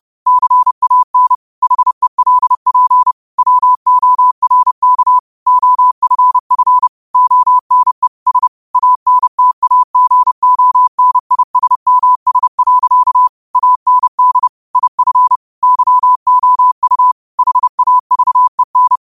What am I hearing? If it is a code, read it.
Morse code, heard as a keyed tone, about 30 wpm, 1000 Hz. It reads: CAN HELP WORK OFF ONE S ANTAGONISMS. AND IF YOU HAVEN